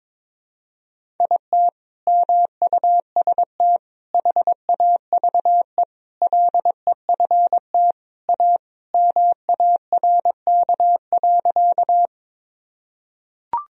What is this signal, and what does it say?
Morse code, heard as a keyed tone, about 22 wpm, 700 Hz. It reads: IT MUST HAVE LEFT A MARK.
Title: Sentences from Top 1000 Words - Review Part 1 - 22wpm